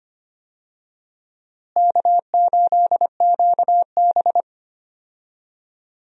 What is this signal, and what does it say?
Morse code, heard as a keyed tone, about 25 wpm, 700 Hz. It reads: K8QB
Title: USA Call Signs - 4 Characters - Repeated 3x - 25wpm